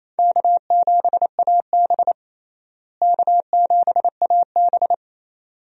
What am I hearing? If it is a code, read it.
Morse code, heard as a keyed tone, about 28 wpm, 700 Hz. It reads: K7AB K7AB
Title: USA Call Signs - 4 Characters - Repeated 2x - 28wpm